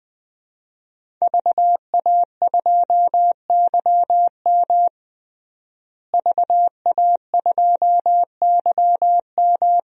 Morse code, heard as a keyed tone, about 21 wpm, 700 Hz. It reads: VA2YM VA2YM
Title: DXCC-100 Call Signs - 5 Characters - Repeated 3x - 20wpm